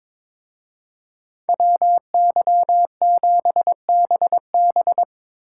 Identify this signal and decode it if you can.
Morse code, heard as a keyed tone, about 25 wpm, 700 Hz. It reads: WY7BB
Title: Speed Racing - USA 2x2 Call Signs 22wpm